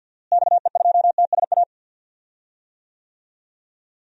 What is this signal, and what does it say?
Morse code, heard as a keyed tone, about 50 wpm, 700 Hz. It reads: XE2TSA
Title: DXCC-100 Call Signs - 6 Characters - Repeated 2x - 50wpm